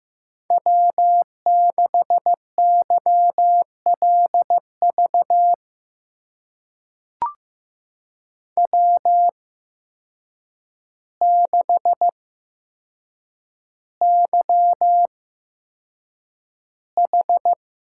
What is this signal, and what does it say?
Morse code, heard as a keyed tone, about 15 wpm, spaced as for 7 wpm, 700 Hz. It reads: W6YLV W6YH
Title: USA Call Signs - 5 Characters - 8x Character Spacing - 15wpm